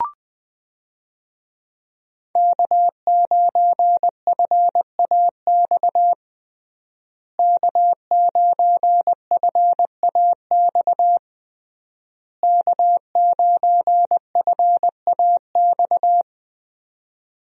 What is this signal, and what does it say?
Morse code, heard as a keyed tone, about 20 wpm, 700 Hz. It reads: K9FAX K9FAX K9FAX
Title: USA Call Signs - 5 Characters - Repeated 3x - 20wpm